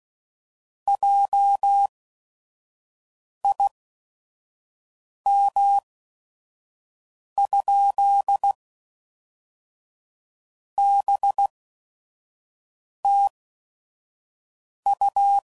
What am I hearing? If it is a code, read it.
Morse code, heard as a keyed tone, about 16 wpm, spaced as for 5 wpm, 800 Hz. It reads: JIM? BTU
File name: CW Code Groups @5WPM / QSOs / WD0FBT 1